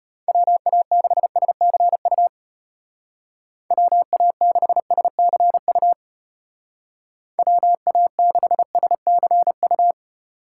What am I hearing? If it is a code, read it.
Morse code, heard as a keyed tone, about 34 wpm, 700 Hz. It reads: WA6SCU WA6SCU WA6SCU